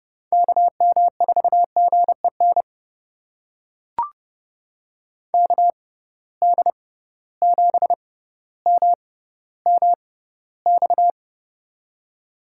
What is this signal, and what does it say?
Morse code, heard as a keyed tone, about 30 wpm, spaced as for 15 wpm, 700 Hz. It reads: KM4GEN KD7MMX
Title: USA Call Signs - 6 Characters - 6x Character Spacing - 30wpm